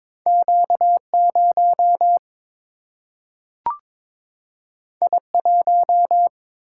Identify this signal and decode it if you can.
Morse code, heard as a keyed tone, about 22 wpm, 700 Hz. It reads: Q 0 I 1
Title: Single-Letter Single-Number 22wpm